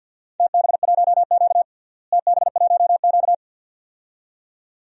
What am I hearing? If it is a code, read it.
Morse code, heard as a keyed tone, about 53 wpm, 700 Hz. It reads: TB1Q TB1Q
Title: DXCC-100 Call Signs - 4 Characters - Repeated 2x - 50wpm